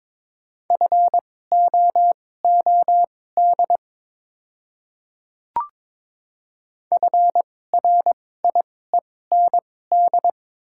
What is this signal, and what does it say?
Morse code, heard as a keyed tone, about 22 wpm, spaced as for 16 wpm, 700 Hz. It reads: FOOD FRIEND
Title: 2x Character Spacing - Top 300 Words - 22wpm